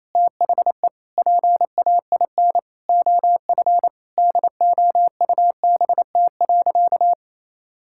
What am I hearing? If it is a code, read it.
Morse code, heard as a keyed tone, about 28 wpm, 700 Hz. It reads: THE PAIN OF DOUBT.
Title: Sentences from Top 1600 Words 28wpm